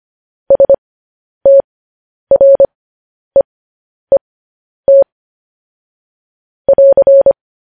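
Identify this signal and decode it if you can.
Morse code, heard as a keyed tone, about 25 wpm, spaced as for 10 wpm, 550 Hz. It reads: STREET <AR>
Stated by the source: Morse code practice sending news headlines at 10 wpm.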